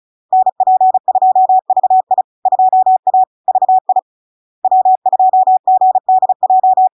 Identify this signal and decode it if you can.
Morse code, heard as a keyed tone, about 36 wpm, 750 Hz. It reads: NP2VI 2A VI W2GDJ
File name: LARK-FD-2022-Log-in-CW-35wpm-750-Hz